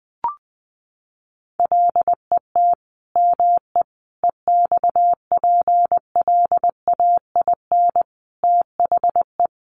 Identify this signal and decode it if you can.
Morse code, heard as a keyed tone, about 20 wpm, 700 Hz. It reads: LET ME EXPLAIN THE
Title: Sentences from Top 1100 Words 20wpm